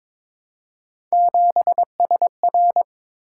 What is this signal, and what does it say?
Morse code, heard as a keyed tone, about 23 wpm, 700 Hz. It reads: 7SR
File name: practice / 22wpm / Three Letter-Number Combinations - 22wpm